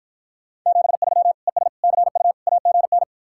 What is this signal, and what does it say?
Morse code, heard as a keyed tone, about 52 wpm, 700 Hz. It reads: 73 ES CU AGN